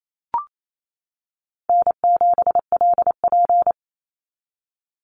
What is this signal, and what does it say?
Morse code, heard as a keyed tone, about 28 wpm, 700 Hz. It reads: N7LP
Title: NA SOTA Activator Callsigns 28wpm